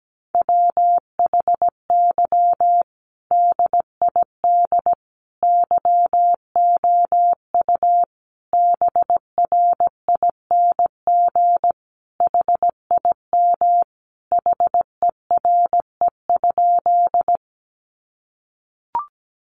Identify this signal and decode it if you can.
Morse code, heard as a keyed tone, about 17 wpm, 700 Hz. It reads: WHY DID YOU BRING HIM HERE?
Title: Sentences from Top 500 Words 17wpm